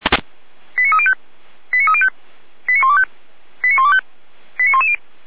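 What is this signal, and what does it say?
Five-tone selective call (ZVEI), heard as quick runs of five tones, tones of about 70 ms. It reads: 89305 89305 89125 89125 89110
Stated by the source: ZVEI five-tone selective call received by SDR.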